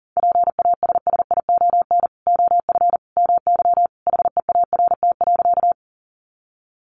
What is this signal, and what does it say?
Morse code, heard as a keyed tone, about 40 wpm, 700 Hz. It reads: PASSION OF MY HEART.
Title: Sentences from Top 1300 Words 40wpm